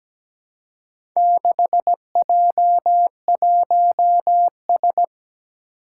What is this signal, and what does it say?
Morse code, heard as a keyed tone, about 17 wpm, 700 Hz. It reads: 6J1S
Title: Four Challenging Letter-Number Combinations 17wpm